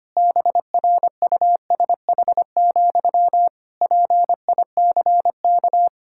Morse code, heard as a keyed tone, about 25 wpm, 700 Hz. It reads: BRUSH, PICK
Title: Pangram Challenge v2 - 25wpm